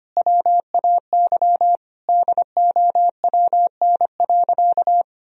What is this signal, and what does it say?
Morse code, heard as a keyed tone, about 25 wpm, 700 Hz. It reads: WAY DOWN.
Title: Sentences from Top 1000 Words - Review Part 1 - 25wpm